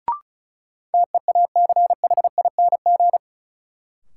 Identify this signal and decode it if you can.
Morse code, heard as a keyed tone, about 36 wpm, 700 Hz. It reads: TEACHING
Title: Sentences from Top 1100 Words - New Word Drill - 35wpm